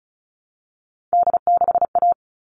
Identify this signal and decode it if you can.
Morse code, heard as a keyed tone, about 34 wpm, 700 Hz. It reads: D6A
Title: All Cut Numbers - 3-Digit 35wpm